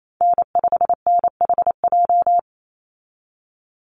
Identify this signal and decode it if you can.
Morse code, heard as a keyed tone, about 28 wpm, 690 Hz. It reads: N5NHJ
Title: Call Signs Repeated 3x - 28wpm